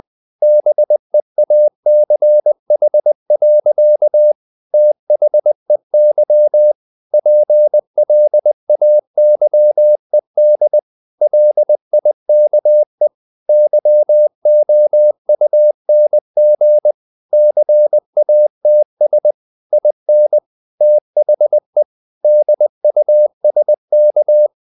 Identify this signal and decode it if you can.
Morse code, heard as a keyed tone, about 20 wpm, 600 Hz. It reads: BEACH. THEY PLAYED LIKE YOUNG CATS IN THE DUSK
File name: the old man and the sea / the old man and the sea part 1 20wpm